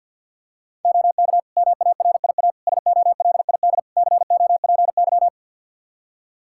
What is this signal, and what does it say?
Morse code, heard as a keyed tone, about 46 wpm, 700 Hz. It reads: OK MARIA SOLID COPY
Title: Practice QSO - Exchange 3 - More Info - 50wpm